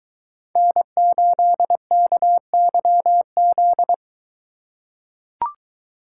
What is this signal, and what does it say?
Morse code, heard as a keyed tone, about 21 wpm, 700 Hz. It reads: N8KYZ